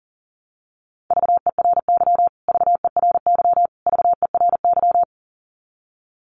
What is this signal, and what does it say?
Morse code, heard as a keyed tone, about 40 wpm, 700 Hz. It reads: VERY VERY VERY
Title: Top 200 Words Repeated 40wpm